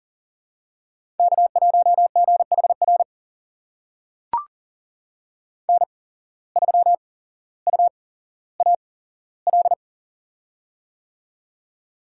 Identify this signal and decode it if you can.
Morse code, heard as a keyed tone, about 38 wpm, spaced as for 17 wpm, 700 Hz. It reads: K1GHR N3UAL
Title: USA Call Signs - 5 Characters - 8x Character Spacing - 40wpm